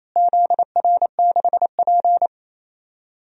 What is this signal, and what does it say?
Morse code, heard as a keyed tone, about 28 wpm, 700 Hz. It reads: ZR6P
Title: DXCC-100 Call Signs - 4 Characters - Repeated 3x - 28wpm